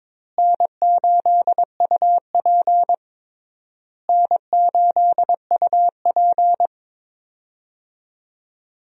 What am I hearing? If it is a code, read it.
Morse code, heard as a keyed tone, about 22 wpm, 700 Hz. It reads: N8UP N8UP
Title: USA Call Signs - 4 Characters - Repeated 3x - 22wpm